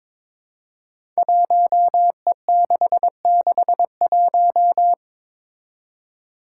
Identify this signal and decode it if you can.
Morse code, heard as a keyed tone, about 22 wpm, 700 Hz. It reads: 1E661
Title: Five Challenging Letter-Number Combinations 22wpm